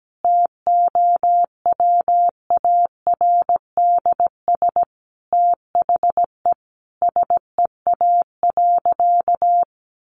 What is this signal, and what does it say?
Morse code, heard as a keyed tone, about 17 wpm, 700 Hz. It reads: TOWARDS THE SEA.
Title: Sentences from Top 1100 Words 17wpm